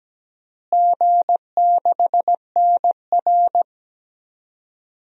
Morse code, heard as a keyed tone, about 17 wpm, 700 Hz. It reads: G6NR
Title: DXCC-100 Call Signs - 4 Characters - Repeated 3x - 17wpm